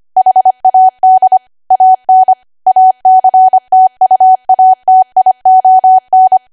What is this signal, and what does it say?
Morse code, heard as a keyed tone, about 25 wpm, 750 Hz. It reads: HAD AN ACTUATION